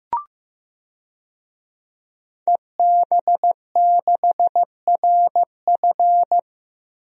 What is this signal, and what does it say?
Morse code, heard as a keyed tone, about 16 wpm, 700 Hz. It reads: EB6RF
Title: DXCC-100 Call Signs - 5 Characters - Repeated 3x - 15wpm